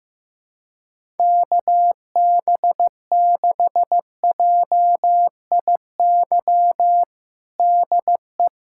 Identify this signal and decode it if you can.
Morse code, heard as a keyed tone, about 16 wpm, 700 Hz. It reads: KB6JIY DE
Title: USA Callsign DE Callsign - 6 Characters - 15wpm